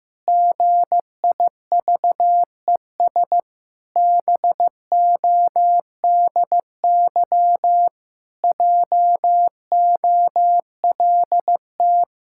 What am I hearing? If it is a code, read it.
Morse code, heard as a keyed tone, about 16 wpm, 700 Hz. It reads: GIVES BODY JOLT
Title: Pangram Challenge v2 - 15wpm